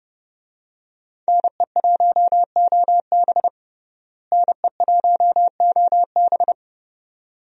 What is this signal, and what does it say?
Morse code, heard as a keyed tone, about 30 wpm, 700 Hz. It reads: NE1OB NE1OB
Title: USA Call Signs - 5 Characters - Repeated 2x - 30wpm